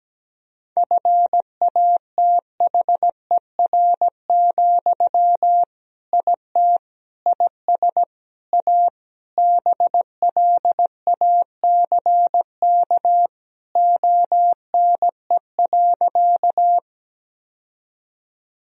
Morse code, heard as a keyed tone, about 17 wpm, 700 Hz. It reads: FATHER, IT IS A BLACK ONE.
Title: Sentences from Top 1000 Words - Review Part 2 - 17wpm